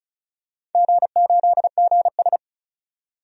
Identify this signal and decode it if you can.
Morse code, heard as a keyed tone, about 34 wpm, 700 Hz. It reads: G8GS
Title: DXCC-100 Call Signs - 4 Characters - 35wpm